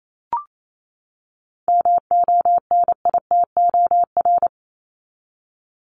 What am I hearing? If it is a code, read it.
Morse code, heard as a keyed tone, about 28 wpm, 700 Hz. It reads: MONITOR
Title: Top 1751-2000 Words 28wpm